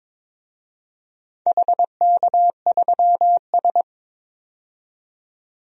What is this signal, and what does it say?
Morse code, heard as a keyed tone, about 22 wpm, 690 Hz. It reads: HK3S
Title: DXCC-100 Call Signs - 4 Characters - 22wpm